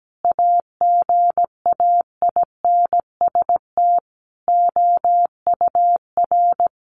Morse code, heard as a keyed tone, about 17 wpm, 700 Hz. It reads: AGAINST OUR
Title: Sentences from Top 1500 Words 17wpm